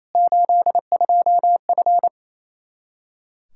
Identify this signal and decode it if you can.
Morse code, heard as a keyed tone, about 27 wpm, 700 Hz. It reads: 82F
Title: Practice QSO - Exchange 2 - Info - 28wpm